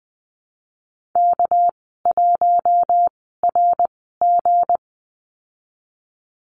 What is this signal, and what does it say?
Morse code, heard as a keyed tone, about 20 wpm, spaced as for 15 wpm, 700 Hz. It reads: K1RG
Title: 213 Lesson 34g - Number 8 - Call Signs 20wpm